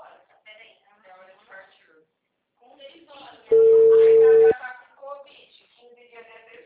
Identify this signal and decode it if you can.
Ringback tone (China, 450 Hz), with speech.